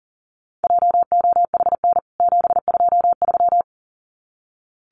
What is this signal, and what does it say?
Morse code, heard as a keyed tone, about 40 wpm, 700 Hz. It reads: JOHN 723
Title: CWT Exchange 40wpm